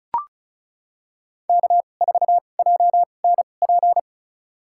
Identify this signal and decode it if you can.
Morse code, heard as a keyed tone, about 36 wpm, spaced as for 25 wpm, 700 Hz. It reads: K4JNP
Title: USA Call Signs - 5 Characters - 2x Character Spacing - 35wpm